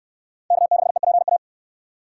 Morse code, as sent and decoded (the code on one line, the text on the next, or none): -.- -.... .--. .-
K6EPA